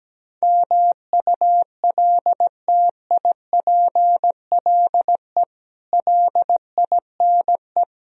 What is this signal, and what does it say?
Morse code, heard as a keyed tone, about 17 wpm, 700 Hz. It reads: MULTIPLE LINE
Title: Speed-Racing - Sets of 5 Words 17wpm